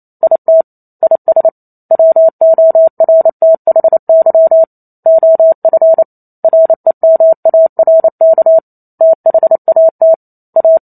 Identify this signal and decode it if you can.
Morse code, heard as a keyed tone, about 28 wpm, 650 Hz. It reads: IT IS WORTHY OF REMARK THAT A